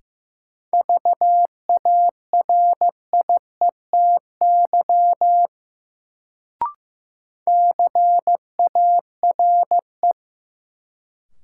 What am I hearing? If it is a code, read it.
Morse code, heard as a keyed tone, about 15 wpm, 700 Hz. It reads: VARIETY CARE
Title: Sets of 3 Words - New Word Drill - 15wpm